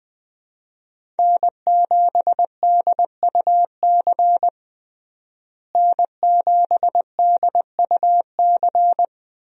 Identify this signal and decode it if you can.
Morse code, heard as a keyed tone, about 20 wpm, 700 Hz. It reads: N7DUC N7DUC
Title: USA Call Signs - 5 Characters - Repeated 3x - 20wpm